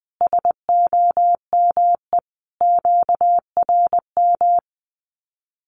Morse code, heard as a keyed tone, about 20 wpm, 700 Hz. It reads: SOME QRM
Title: Common QSO Phrases 20wpm